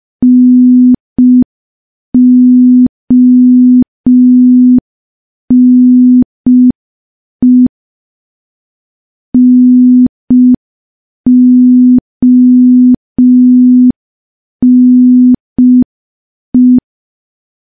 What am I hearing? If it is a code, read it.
Morse code, heard as a keyed tone, about 5 wpm, 250 Hz. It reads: NONE NONE
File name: morse / morse-training-05WPM